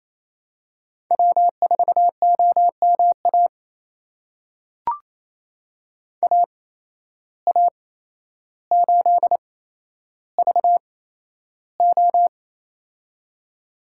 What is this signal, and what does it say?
Morse code, heard as a keyed tone, about 28 wpm, spaced as for 12 wpm, 700 Hz. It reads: W4OMA AA8VO
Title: USA Call Signs - 5 Characters - 8x Character Spacing - 28wpm